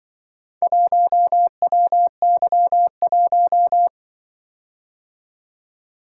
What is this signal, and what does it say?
Morse code, heard as a keyed tone, about 24 wpm, 700 Hz. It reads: 1WY1
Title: Speed-Racing - Four Letter-Number Combinations - 20wpm